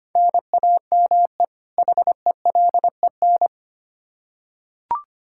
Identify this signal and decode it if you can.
Morse code, heard as a keyed tone, about 25 wpm, 700 Hz. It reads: NAME HELEN
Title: Practice QSO - Exchange 1 - Beginning - 25wpm